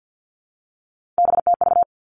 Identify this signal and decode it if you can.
Morse code, heard as a keyed tone, about 50 wpm, 700 Hz. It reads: BTV
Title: Three Challenging Letter-Number Combinations 50wpm